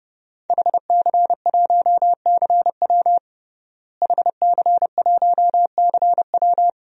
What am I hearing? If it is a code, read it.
Morse code, heard as a keyed tone, about 30 wpm, 700 Hz. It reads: HC1CW HC1CW
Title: DXCC-100 Call Signs - 5 Characters - Repeated 3x - 30wpm